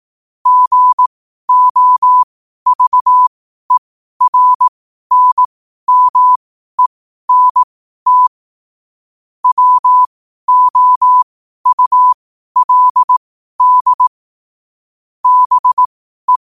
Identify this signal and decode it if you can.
Morse code, heard as a keyed tone, about 18 wpm, spaced as for 12 wpm, 1000 Hz. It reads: GOVERNMENT WOULD BE
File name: qotdfiles / 12wpm